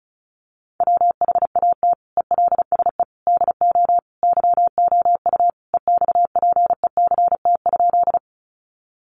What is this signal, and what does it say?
Morse code, heard as a keyed tone, about 34 wpm, 700 Hz. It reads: WHAT ELSE DO YOU EXPECT?
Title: Sentences from Top 1100 Words 35wpm